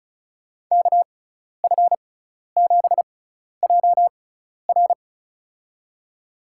Morse code, heard as a keyed tone, about 36 wpm, spaced as for 12 wpm, 700 Hz. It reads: KF7JR
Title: USA Call Signs - 5 Characters - 6x Character Spacing - 35wpm